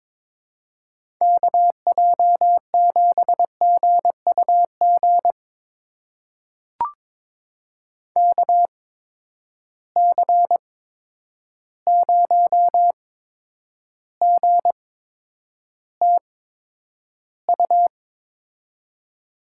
Morse code, heard as a keyed tone, about 21 wpm, spaced as for 9 wpm, 700 Hz. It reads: KJ7GUG KC0GTU
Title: USA Call Signs - 6 Characters - 8x Character Spacing - 22wpm